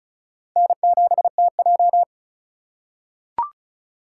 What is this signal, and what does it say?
Morse code, heard as a keyed tone, about 36 wpm, 700 Hz. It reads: N7TJ
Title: USA Call Signs - 4 Characters - 4x Character Spacing - 35wpm